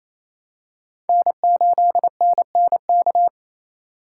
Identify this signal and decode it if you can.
Morse code, heard as a keyed tone, about 27 wpm, 700 Hz. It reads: N8NNK
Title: USA Call Signs - 3x Character Spacing - 28wpm